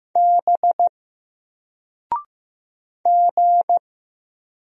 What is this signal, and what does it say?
Morse code, heard as a keyed tone, about 15 wpm, 700 Hz. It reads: B G
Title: Single Letter-Number 15wpm